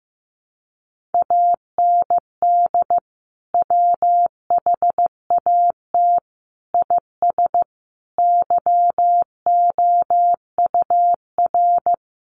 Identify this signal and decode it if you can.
Morse code, heard as a keyed tone, about 15 wpm, 700 Hz. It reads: AND WHAT IS YOUR
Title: Sentences from Top 1200 Words 15wpm